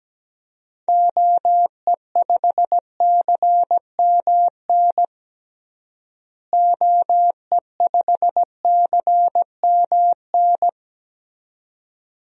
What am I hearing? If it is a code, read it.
Morse code, heard as a keyed tone, about 17 wpm, 700 Hz. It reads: OE5CMN OE5CMN
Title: DXCC-100 Call Signs - 6 Characters - Repeated 2x - 17wpm